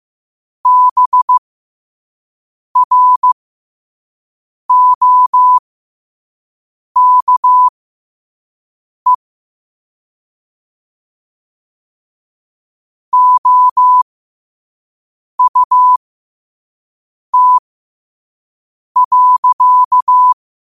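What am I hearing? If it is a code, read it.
Morse code, heard as a keyed tone, about 15 wpm, spaced as for 5 wpm, 1000 Hz. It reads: BROKE OUT.
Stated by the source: Morse code sent at 5 wpm.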